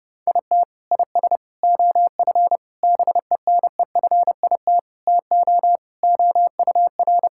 Morse code, heard as a keyed tone, about 30 wpm, 700 Hz. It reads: IT IS OF BENEFIT TO OUR
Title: Sentences from Top 1000 Words - Review Part 6 - 30wpm